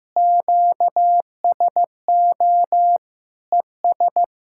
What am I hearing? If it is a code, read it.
Morse code, heard as a keyed tone, about 15 wpm, 700 Hz. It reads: QSO ES
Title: Speed-Racing Common QSO Phrases Encore - 15wpm